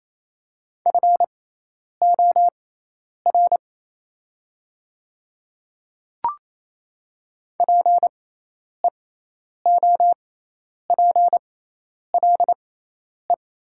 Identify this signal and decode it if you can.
Morse code, heard as a keyed tone, about 28 wpm, spaced as for 10 wpm, 700 Hz. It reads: FOR PEOPLE
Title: 6x Character Spacing - Top 200 Words - 28wpm